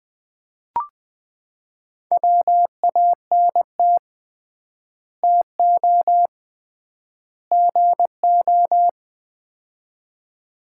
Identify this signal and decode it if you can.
Morse code, heard as a keyed tone, about 20 wpm, 700 Hz. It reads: WANT TO GO
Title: 3x Word-Spacing - Sets of 3 Words - 20wpm